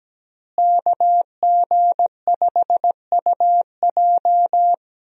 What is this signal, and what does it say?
Morse code, heard as a keyed tone, about 17 wpm, 700 Hz. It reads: KG5UJ
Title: USA Call Signs - 5 Characters - Repeated 3x - 17wpm